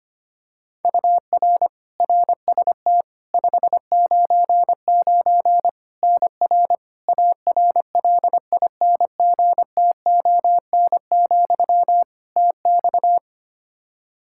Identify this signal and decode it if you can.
Morse code, heard as a keyed tone, about 25 wpm, 700 Hz. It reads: UR RST 599 NR ARLINGTON, TX
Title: Practice QSO - Exchange 1 - Beginning - 25wpm